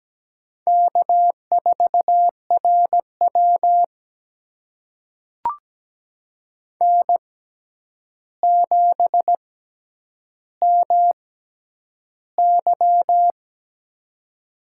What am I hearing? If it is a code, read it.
Morse code, heard as a keyed tone, about 17 wpm, spaced as for 9 wpm, 700 Hz. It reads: K4RW N7MY
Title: USA Call Signs - 6x Character Spacing - 17wpm